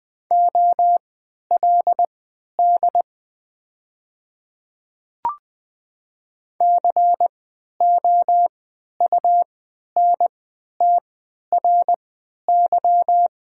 Morse code, heard as a keyed tone, about 20 wpm, spaced as for 11 wpm, 690 Hz. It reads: OLD COUNTRY
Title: 3x Character Spacing - Top 300 Words - 20wpm